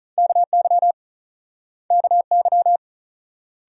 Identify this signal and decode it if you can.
Morse code, heard as a keyed tone, about 50 wpm, 700 Hz. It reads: KY KY